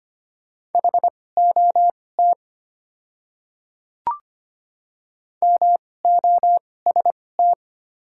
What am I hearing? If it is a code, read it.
Morse code, heard as a keyed tone, about 25 wpm, spaced as for 18 wpm, 700 Hz. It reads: HOT MOST
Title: 2x Character Spacing - Top 100 Words - 25wpm